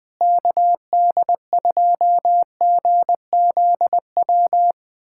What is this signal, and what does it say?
Morse code, heard as a keyed tone, about 21 wpm, 700 Hz. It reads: KD2GZW
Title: USA Callsign DE Callsign - 6 Characters - 20wpm